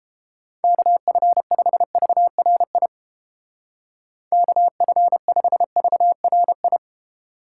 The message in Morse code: KF5VRI KF5VRI

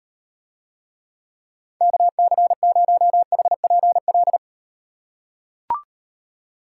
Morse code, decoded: KC0HPL

